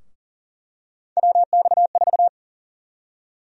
Morse code, heard as a keyed tone, about 40 wpm, 700 Hz. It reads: WX4
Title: USA Call Sign Prefixes - 40wpm